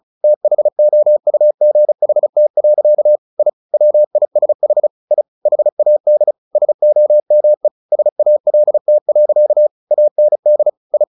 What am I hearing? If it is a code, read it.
Morse code, heard as a keyed tone, about 34 wpm, 600 Hz. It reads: THOUGHT. I WISH I HAD SOME SALT. AND I
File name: the old man and the sea / the old man and the sea 1 35wpm